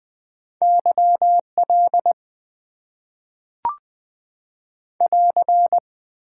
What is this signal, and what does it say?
Morse code, heard as a keyed tone, about 20 wpm, 700 Hz. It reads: YL <AR>
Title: Top 90 QSO Elements - 8x Character Spacing - 20wpm